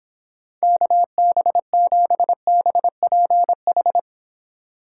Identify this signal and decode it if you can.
Morse code, heard as a keyed tone, about 29 wpm, 700 Hz. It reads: KB7BPH